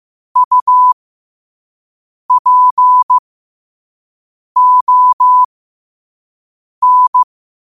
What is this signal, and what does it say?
Morse code, heard as a keyed tone, about 15 wpm, spaced as for 5 wpm, 1000 Hz. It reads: UPON
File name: qotdfiles / 5wpm